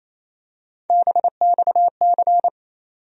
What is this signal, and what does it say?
Morse code, heard as a keyed tone, about 27 wpm, 700 Hz. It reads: BXC
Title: USA Call Sign Postfixes - 3 Letters 28wpm